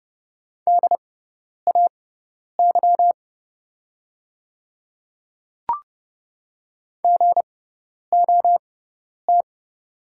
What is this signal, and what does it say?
Morse code, heard as a keyed tone, about 30 wpm, spaced as for 10 wpm, 700 Hz. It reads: DAY GOT
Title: 6x Character Spacing - Top 500 Words - 30wpm